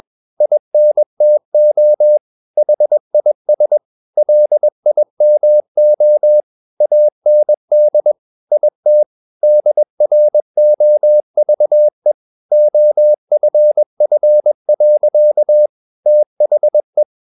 Morse code, heard as a keyed tone, about 21 wpm, 600 Hz. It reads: INTO HIS LIMO AND IT DROVE OFF. THE
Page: CW med Gnister